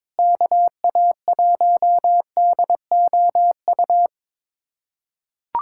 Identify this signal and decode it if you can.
Morse code, heard as a keyed tone, about 21 wpm, 700 Hz. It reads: KA1DOU
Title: USA Callsign DE Callsign 22wpm